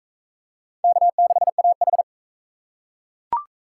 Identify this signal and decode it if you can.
Morse code, heard as a keyed tone, about 40 wpm, 700 Hz. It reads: K6AH